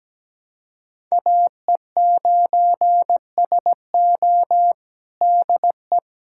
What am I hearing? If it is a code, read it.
Morse code, heard as a keyed tone, about 17 wpm, 700 Hz. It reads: AE9SO DE
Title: USA Callsign DE Callsign 17wpm